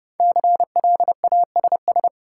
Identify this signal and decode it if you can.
Morse code, heard as a keyed tone, about 32 wpm, 700 Hz. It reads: CLASS
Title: Sentences from Top 1000 Words - Review Part 2 - New Word Drill - 30wpm